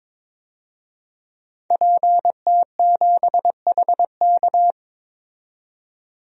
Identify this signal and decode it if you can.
Morse code, heard as a keyed tone, about 22 wpm, 700 Hz. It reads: PT7HK
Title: DXCC-100 Call Signs - 5 Characters - 22wpm